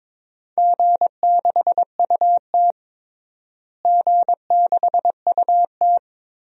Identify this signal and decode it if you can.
Morse code, heard as a keyed tone, about 22 wpm, 700 Hz. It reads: G6UT G6UT
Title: DXCC-100 Call Signs - 4 Characters - Repeated 3x - 22wpm